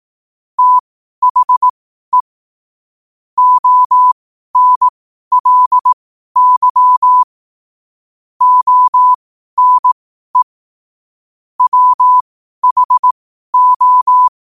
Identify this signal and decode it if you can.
Morse code, heard as a keyed tone, about 18 wpm, spaced as for 12 wpm, 1000 Hz. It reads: THE ONLY ONE WHO
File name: qotdfiles / 12wpm